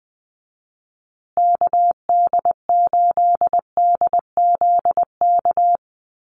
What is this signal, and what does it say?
Morse code, heard as a keyed tone, about 20 wpm, 700 Hz. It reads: KD8DZK
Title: USA Call Signs Repeated 2x 20wpm